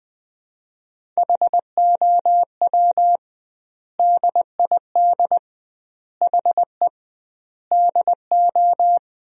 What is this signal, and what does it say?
Morse code, heard as a keyed tone, about 20 wpm, 700 Hz. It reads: HOW DID HE DO
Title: 2x Word-Spacing - Sentences from Top 100 Words - 20wpm